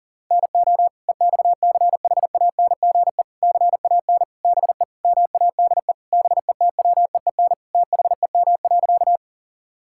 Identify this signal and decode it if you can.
Morse code, heard as a keyed tone, about 40 wpm, 700 Hz. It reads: NO EXCHANGE CAN BE MADE BETWEEN THEM.
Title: Sentences from Top 1000 Words - Review Part 6 - 40wpm